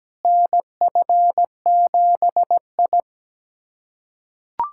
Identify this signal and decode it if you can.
Morse code, heard as a keyed tone, about 17 wpm, 700 Hz. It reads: NF7I
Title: USA Callsign DE Callsign - 4 Characters - 17wpm